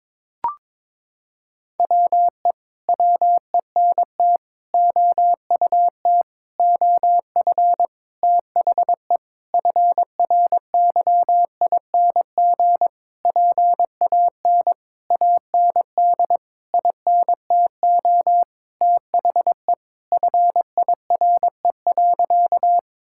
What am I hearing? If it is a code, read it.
Morse code, heard as a keyed tone, about 22 wpm, 700 Hz. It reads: WE WENT OUT OF THE FRYING PAN AND INTO THE FIRE.
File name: Sentences with 9-Word Phrases - 22wpm